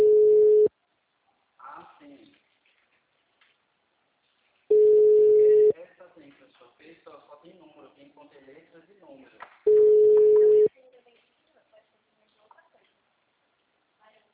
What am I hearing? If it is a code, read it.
Ringback tone (Europe, 425 Hz), with speech.